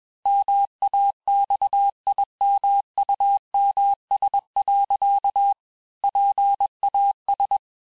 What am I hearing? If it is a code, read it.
Morse code, heard as a keyed tone, about 21 wpm, 800 Hz. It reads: MAXIMUMS. PAS
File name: teste-FIN-de-QSO-vitesse-20mpm